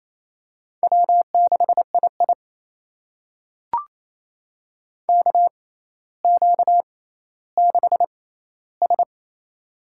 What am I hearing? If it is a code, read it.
Morse code, heard as a keyed tone, about 28 wpm, spaced as for 14 wpm, 700 Hz. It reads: W6II KQ6S
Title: USA Call Signs - 4 Characters - 6x Character Spacing - 28wpm